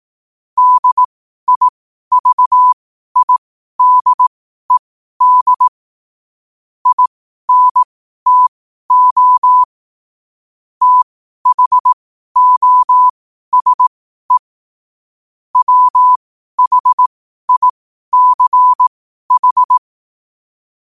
Morse code, heard as a keyed tone, about 18 wpm, spaced as for 12 wpm, 1000 Hz. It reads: DIVIDED INTO THOSE WHICH
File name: qotdfiles / 12wpm